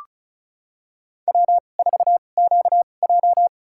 Morse code, heard as a keyed tone, about 36 wpm, spaced as for 25 wpm, 700 Hz. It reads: W4QJ
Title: USA Call Signs - 2x Character Spacing - 35wpm